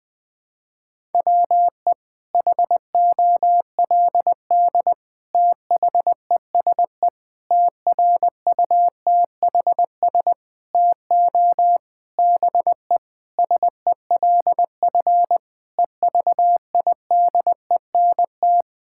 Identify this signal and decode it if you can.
Morse code, heard as a keyed tone, about 20 wpm, 700 Hz. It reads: WE HOLD THESE TRUTHS TO BE SELF EVIDENT
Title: Sentences with 9-Word Phrases - 20wpm